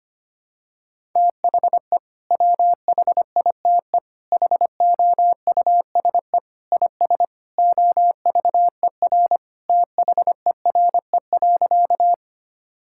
Morse code, heard as a keyed tone, about 25 wpm, 700 Hz. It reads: THE WHITE HOUSE IS OVER THERE.
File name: practice / 25wpm / Sentences with 3-Word Phrases - 25wpm